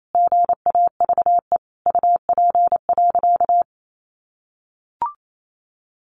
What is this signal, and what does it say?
Morse code, heard as a keyed tone, about 28 wpm, 700 Hz. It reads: GAVE UP.
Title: Sentences from Top 1500 Words 28wpm